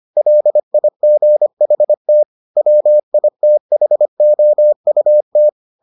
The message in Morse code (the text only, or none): LIGHT WITHOUT